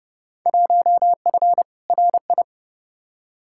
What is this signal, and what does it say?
Morse code, heard as a keyed tone, about 30 wpm, 700 Hz. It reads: 1F RI
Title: Field Day Exchanges Encore - 30wpm